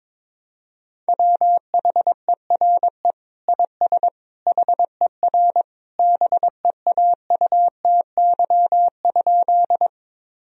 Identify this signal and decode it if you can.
Morse code, heard as a keyed tone, about 22 wpm, 700 Hz. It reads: WHERE IS HER BEAUTY?
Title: Sentences from Top 1000 Words - Review Part 2 - 22wpm